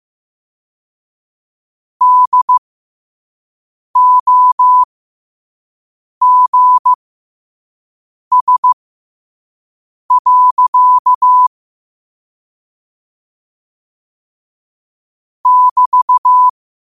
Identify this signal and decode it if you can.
Morse code, heard as a keyed tone, about 15 wpm, spaced as for 5 wpm, 1000 Hz. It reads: DOGS. <BT>
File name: qotdfiles / 5wpm